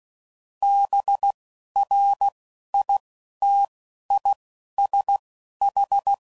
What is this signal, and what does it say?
Morse code, heard as a keyed tone, about 16 wpm, spaced as for 11 wpm, 800 Hz. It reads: BRITISH
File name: Suffixes / General / ISHs